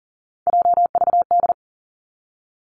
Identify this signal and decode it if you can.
Morse code, heard as a keyed tone, about 38 wpm, 700 Hz. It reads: JVD
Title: Three Challenging Letter-Number Combinations 40wpm